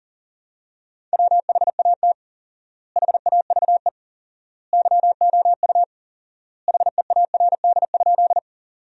Morse code, heard as a keyed tone, about 40 wpm, 700 Hz. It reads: WHAT HAVE YOU HEARD?
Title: 4x Word-Spacing - Sentences from Top 400 Words - 40wpm